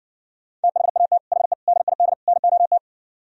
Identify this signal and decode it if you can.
Morse code, heard as a keyed tone, about 59 wpm, 700 Hz. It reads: THAT HE DID NOT